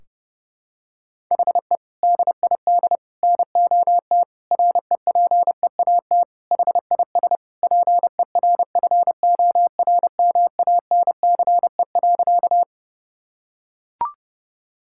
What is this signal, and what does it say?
Morse code, heard as a keyed tone, about 30 wpm, 700 Hz. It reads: HE DID NOT REPEAT HIS PERFORMANCE.
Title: Sentences from Top 1600 Words 30wpm